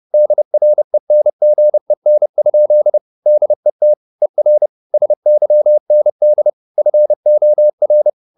Morse code, heard as a keyed tone, about 30 wpm, 600 Hz. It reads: DRENGEN? DET ER SYND FOR
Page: Ferie 30 wpm